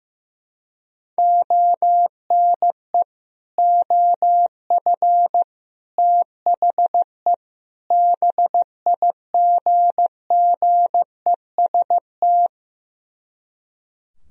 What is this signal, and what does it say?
Morse code, heard as a keyed tone, about 15 wpm, 700 Hz. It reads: ONE OF THE BIGGEST
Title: Speed-Racing - Sets of 4 Words 15wpm